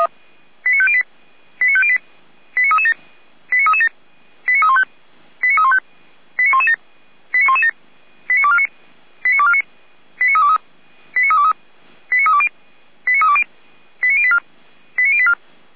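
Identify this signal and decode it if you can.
Five-tone selective call (ZVEI), heard as quick runs of five tones, tones of about 70 ms. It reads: 89508 89508 89337 89337 89315 89315 89117 89117 89249 89249 89323 89323 89320 89320 89084 89084
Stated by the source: ZVEI selective call 89167 received on an SDR by the Orlamünde fire brigade.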